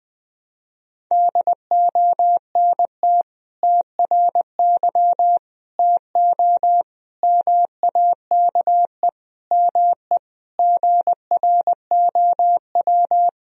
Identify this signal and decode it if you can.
Morse code, heard as a keyed tone, about 20 wpm, 700 Hz. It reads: DONT TRY TO MAKE ME GROW
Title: Sentences from Top 1000 Words - Review Part 1 - 20wpm